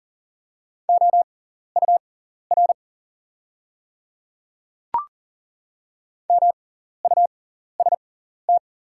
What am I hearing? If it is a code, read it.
Morse code, heard as a keyed tone, about 37 wpm, spaced as for 14 wpm, 690 Hz. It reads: OUR MUST
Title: 6x Character Spacing - Top 300 Words - 40wpm